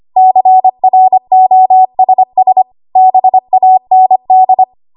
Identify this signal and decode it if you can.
Morse code, heard as a keyed tone, about 25 wpm, 750 Hz. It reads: CROSS BAND
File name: Archive / 25 WPM / 25WPM